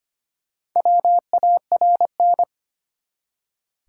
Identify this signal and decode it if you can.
Morse code, heard as a keyed tone, about 25 wpm, 700 Hz. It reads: WARN
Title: Four Letter Words - 25wpm